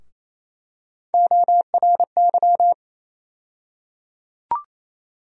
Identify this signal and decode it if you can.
Morse code, heard as a keyed tone, about 28 wpm, 700 Hz. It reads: ORY
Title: USA Call Sign Postfixes - 3 Letters 28wpm